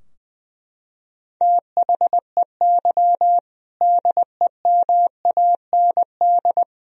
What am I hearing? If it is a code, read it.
Morse code, heard as a keyed tone, about 20 wpm, 700 Hz. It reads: THEY DEMAND
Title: Sentences from Top 1000 Words - Review Part 3 - 20wpm